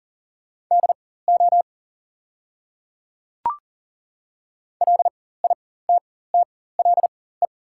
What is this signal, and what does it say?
Morse code, heard as a keyed tone, about 40 wpm, spaced as for 19 wpm, 700 Hz. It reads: DO LITTLE
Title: 4x Character Spacing - Top 300 Words - 40wpm